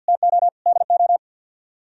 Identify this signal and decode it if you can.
Morse code, heard as a keyed tone, about 62 wpm, 700 Hz. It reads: TO DO
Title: Speed-Racing - Sets of 3 Words 50wpm